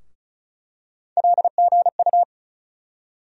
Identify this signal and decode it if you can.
Morse code, heard as a keyed tone, about 36 wpm, 700 Hz. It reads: LGU
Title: USA Call Sign Postfixes - 3 Letters 35wpm